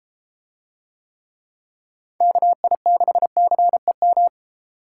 Morse code, heard as a keyed tone, about 32 wpm, 700 Hz. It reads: KI6CEM